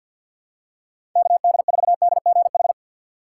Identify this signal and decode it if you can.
Morse code, heard as a keyed tone, about 53 wpm, 700 Hz. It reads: KD4DGH
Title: USA Call Signs - 6 Characters - 3x Character Spacing - 50wpm